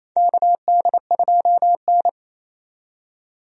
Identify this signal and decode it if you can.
Morse code, heard as a keyed tone, about 28 wpm, 700 Hz. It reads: KD2N
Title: USA Call Signs - 4 Characters - 28wpm